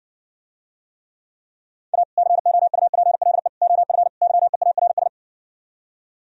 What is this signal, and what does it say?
Morse code, heard as a keyed tone, about 67 wpm, 700 Hz. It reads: A COUPLE OF YEARS